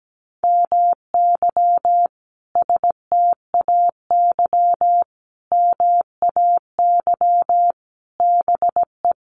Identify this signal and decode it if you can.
Morse code, heard as a keyed tone, about 17 wpm, 700 Hz. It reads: MY STAY MAY BE
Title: Sentences from Top 1000 Words 17wpm